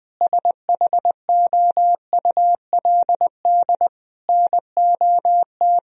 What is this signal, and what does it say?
Morse code, heard as a keyed tone, about 20 wpm, 700 Hz. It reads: SHOULD NOT
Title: Sentences from Top 1000 Words - Review Part 3 - 20wpm